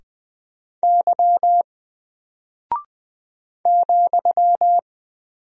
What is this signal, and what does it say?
Morse code, heard as a keyed tone, about 20 wpm, 700 Hz. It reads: Y ,
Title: Single Character (Letter:Number:Punctuation:Prosign) Alt 20wpm